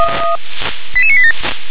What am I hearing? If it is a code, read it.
Five-tone selective call (ZVEI), heard as quick runs of five tones, tones of about 70 ms. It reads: 88987